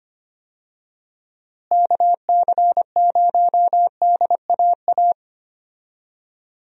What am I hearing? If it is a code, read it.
Morse code, heard as a keyed tone, about 25 wpm, 700 Hz. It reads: KC0DAA
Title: USA Call Signs - 6 Characters - Repeated 3x - 25wpm